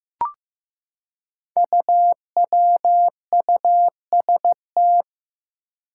Morse code, heard as a keyed tone, about 15 wpm, 700 Hz. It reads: UWUST
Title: Five Challenging Letter-Number Combinations 15wpm